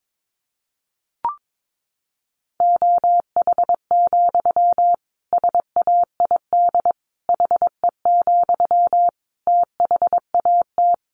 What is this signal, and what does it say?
Morse code, heard as a keyed tone, about 22 wpm, 700 Hz. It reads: OH, SAID HE, THAT
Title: Sentences from Top 500 Words 22wpm